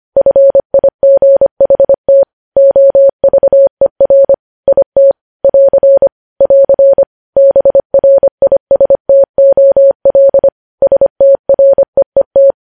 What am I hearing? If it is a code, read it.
Morse code, heard as a keyed tone, about 25 wpm, 550 Hz. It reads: FIGHT OVER IT <AR> <AR> BRISTOL STREET
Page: news headlines in morse code at 25 wpm